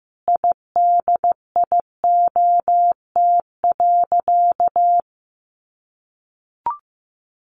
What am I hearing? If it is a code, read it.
Morse code, heard as a keyed tone, about 15 wpm, 700 Hz. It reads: IDIOT.
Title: A Sound Of Thunder 15wpm